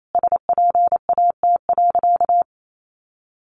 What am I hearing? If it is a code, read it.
Morse code, heard as a keyed tone, about 28 wpm, 700 Hz. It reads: SPAT.
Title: A Sound Of Thunder 28wpm